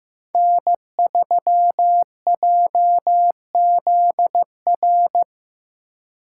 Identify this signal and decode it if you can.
Morse code, heard as a keyed tone, about 18 wpm, 700 Hz. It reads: N3JZR
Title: Speed Racing - USA 1x3 Call Signs 15wpm